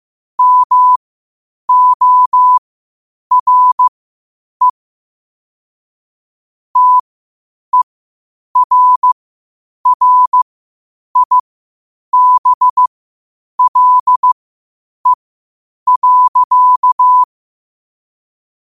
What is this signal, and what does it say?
Morse code, heard as a keyed tone, about 15 wpm, spaced as for 8 wpm, 1000 Hz. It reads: MORE TERRIBLE.
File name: qotdfiles / 8wpm